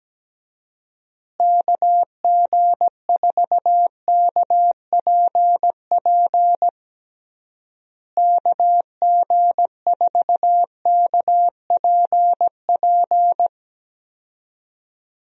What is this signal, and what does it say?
Morse code, heard as a keyed tone, about 17 wpm, 700 Hz. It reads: KG4KPP KG4KPP
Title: USA Call Signs - 6 Characters - Repeated 3x - 17wpm